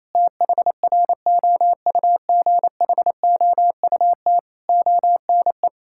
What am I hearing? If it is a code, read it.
Morse code, heard as a keyed tone, about 28 wpm, 700 Hz. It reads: THROUGHOUT ONE
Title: Sentences from Top 1000 Words - Review Part 5 - 28wpm